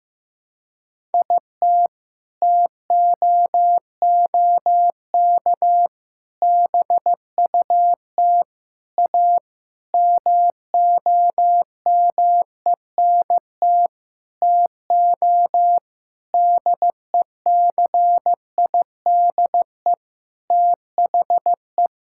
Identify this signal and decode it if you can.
Morse code, heard as a keyed tone, about 15 wpm, 700 Hz. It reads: IT TOOK BUT A MOMENT TO DECIDE THE
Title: Sentences from Top 1600 Words 15wpm